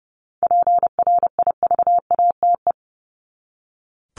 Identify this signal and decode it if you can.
Morse code, heard as a keyed tone, about 30 wpm, 700 Hz. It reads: PRIVATE
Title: Top 751-1000 Words 30wpm